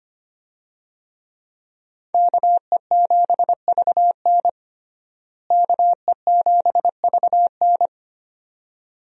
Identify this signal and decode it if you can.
Morse code, heard as a keyed tone, about 25 wpm, 700 Hz. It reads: KE7VN KE7VN
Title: USA Call Signs - 5 Characters - Repeated 2x - 25wpm